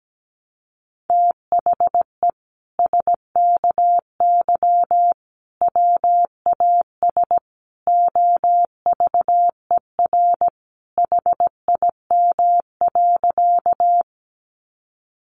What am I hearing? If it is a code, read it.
Morse code, heard as a keyed tone, about 17 wpm, 700 Hz. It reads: THE SKY WAS OVER HIM.
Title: Sentences from Top 1200 Words 17wpm